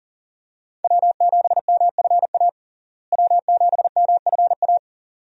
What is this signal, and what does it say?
Morse code, heard as a keyed tone, about 40 wpm, 700 Hz. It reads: W7MFA W7MFA
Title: USA Call Signs - 5 Characters - Repeated 2x - 40wpm